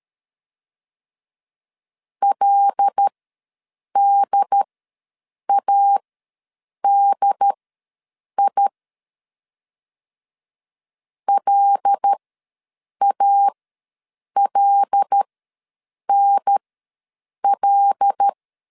Morse code, heard as a keyed tone, about 13 wpm, spaced as for 7 wpm, 800 Hz. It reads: LDADI LALNL
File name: LEZIONI / 5 LBVC / 5 L ws 12wpm 2958